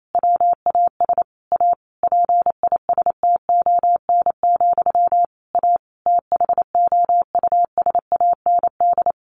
Morse code, heard as a keyed tone, about 28 wpm, 700 Hz. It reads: WAS A PISTON, A THOUSAND
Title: A Sound Of Thunder 28wpm